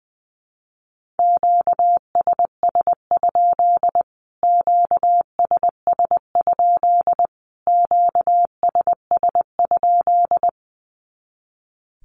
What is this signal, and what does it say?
Morse code, heard as a keyed tone, about 20 wpm, 700 Hz. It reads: QSS? QSS? QSS?
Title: Q-Codes Repeated 3x 20wpm